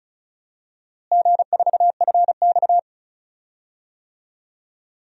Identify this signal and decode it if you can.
Morse code, heard as a keyed tone, about 34 wpm, 700 Hz. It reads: G4FX
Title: DXCC-100 Call Signs - 4 Characters - Repeated 2x - 35wpm